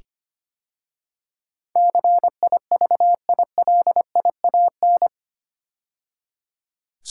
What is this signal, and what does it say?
Morse code, heard as a keyed tone, about 25 wpm, 700 Hz. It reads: CIVILIAN
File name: 8 Letter Words - First 3 Letters Given - 25wpm